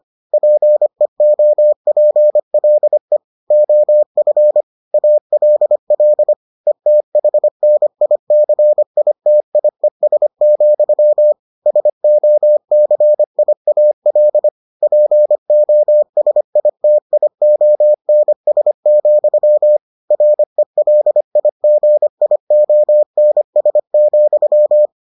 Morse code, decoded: PEOPLE OF ALL ETHNICITIES, SOCIAL POSITIONS, RELIGIONS,

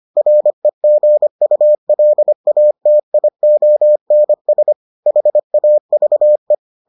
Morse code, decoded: REGULATIONS HAVE

